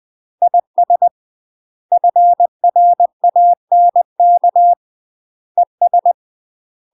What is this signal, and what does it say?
Morse code, heard as a keyed tone, about 20 wpm, 700 Hz. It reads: IS FRANK ES
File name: UK MASTER QSOs TRAINING / 20WPM / UK MASTER QSOs 0051